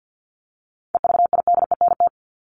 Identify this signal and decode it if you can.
Morse code, heard as a keyed tone, about 50 wpm, 700 Hz. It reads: EVIDENT
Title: LY Suffix - Top 100 Words with Base Words 50wpm